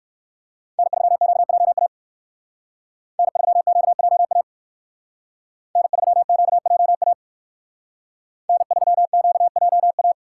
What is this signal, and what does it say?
Morse code, heard as a keyed tone, about 58 wpm, 700 Hz. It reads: N3QJA N3QJA N3QJA N3QJA